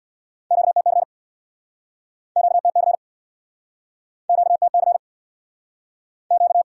Morse code, heard as a keyed tone, about 63 wpm, 700 Hz. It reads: QTC QTC QTC Q